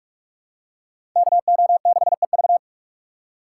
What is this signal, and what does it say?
Morse code, heard as a keyed tone, about 43 wpm, 700 Hz. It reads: KO6EV